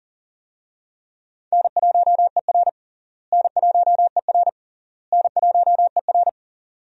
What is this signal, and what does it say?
Morse code, heard as a keyed tone, about 40 wpm, 700 Hz. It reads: N1ER N1ER N1ER
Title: USA Call Signs - 4 Characters - Repeated 3x - 40wpm